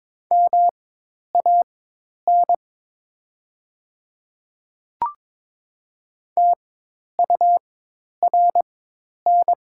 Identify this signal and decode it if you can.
Morse code, heard as a keyed tone, about 22 wpm, spaced as for 10 wpm, 700 Hz. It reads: MAN TURN
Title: 4x Character Spacing - Top 200 Words - 22wpm